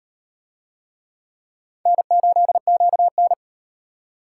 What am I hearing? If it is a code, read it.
Morse code, heard as a keyed tone, about 38 wpm, 700 Hz. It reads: N8QN